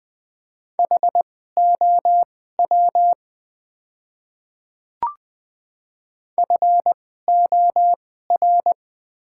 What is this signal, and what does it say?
Morse code, heard as a keyed tone, about 20 wpm, spaced as for 14 wpm, 700 Hz. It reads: HOW FOR
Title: 2x Character Spacing - Top 100 Words - 20wpm